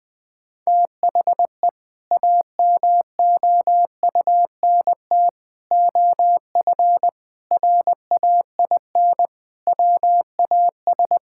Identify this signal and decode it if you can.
Morse code, heard as a keyed tone, about 20 wpm, 700 Hz. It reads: THE AMOUNT OF RAIN WAS
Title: Sentences with 3-Word Phrases - Spoken Beginning - 20wpm